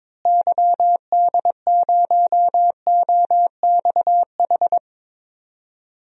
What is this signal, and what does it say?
Morse code, heard as a keyed tone, about 22 wpm, 700 Hz. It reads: YD0OXH
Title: DXCC-100 Call Signs - 6 Characters - Repeated 2x - 22wpm